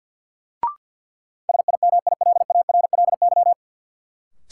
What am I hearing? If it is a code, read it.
Morse code, heard as a keyed tone, about 51 wpm, 700 Hz. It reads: SIMILARLY
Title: Sentences from Top 1400 Words - New Word Drill Alt - 50wpm